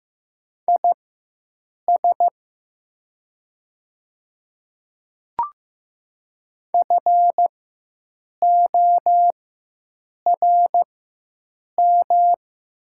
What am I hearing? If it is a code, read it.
Morse code, heard as a keyed tone, about 15 wpm, spaced as for 7 wpm, 700 Hz. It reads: IS FORM
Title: 4x Character Spacing - Top 200 Words - 15wpm